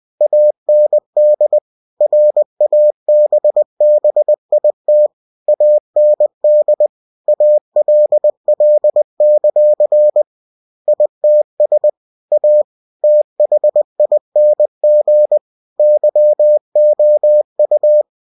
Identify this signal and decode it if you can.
Morse code, heard as a keyed tone, about 20 wpm, 600 Hz. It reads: AND RABBIT AND ALL; ITS A THING YOU
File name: cw-books / winnie-the-pooh / winnie-20wpm-page-0081